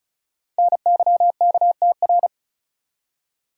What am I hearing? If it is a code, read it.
Morse code, heard as a keyed tone, about 36 wpm, 700 Hz. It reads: NYKTR
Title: Code Groups - 5 Letters - 35wpm